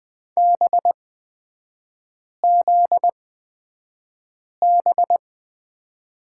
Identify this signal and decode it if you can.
Morse code, heard as a keyed tone, about 20 wpm, spaced as for 5 wpm, 700 Hz. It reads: BZB
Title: 243 Lesson 39e - Letter z - Three Characters 20wpm